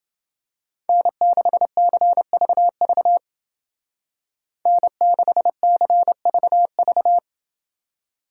Morse code, heard as a keyed tone, about 28 wpm, 700 Hz. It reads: N6CVV N6CVV